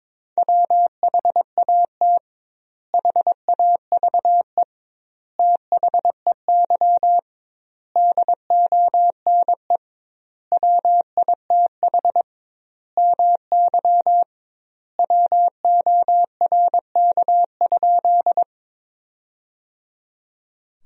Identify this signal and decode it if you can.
Morse code, heard as a keyed tone, about 22 wpm, 700 Hz. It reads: WHAT HAVE THEY DONE WITH MY WORK?
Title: 2x Word-Spacing - Sentences from Top 500 Words - 22wpm